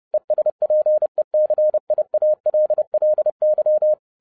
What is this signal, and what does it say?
Morse code, heard as a keyed tone, about 30 wpm, 610 Hz. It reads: ESPECIALLY
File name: 2012-01-mp3 / n12-I DIDNT MISS THE SOLAR CYCLE 24-30wpm